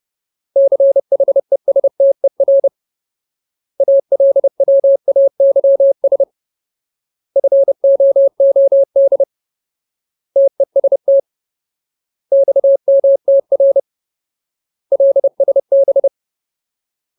Morse code, heard as a keyed tone, about 30 wpm, 550 Hz. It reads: CHESTER ALWAYS FOOD TEST XMTR LSB
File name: Ultimate Word List / 30WPM Ultimate